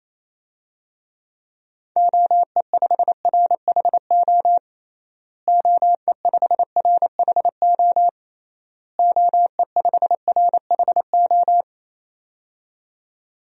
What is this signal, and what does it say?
Morse code, heard as a keyed tone, about 28 wpm, 700 Hz. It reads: OE5RHO OE5RHO OE5RHO
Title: DXCC-100 Call Signs - 6 Characters - Repeated 3x - 28wpm